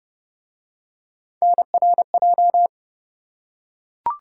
Morse code, heard as a keyed tone, about 29 wpm, 700 Hz. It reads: NRJ